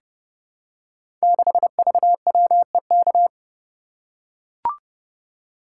Morse code, heard as a keyed tone, about 30 wpm, 700 Hz. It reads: 6VWEK
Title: Code Groups - 5 Letter:Numbers - 30wpm